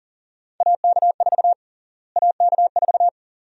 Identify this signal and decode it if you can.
Morse code, heard as a keyed tone, about 40 wpm, 700 Hz. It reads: AK4 AK4
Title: USA Call Sign Prefixes - Repeated 2x - 40wpm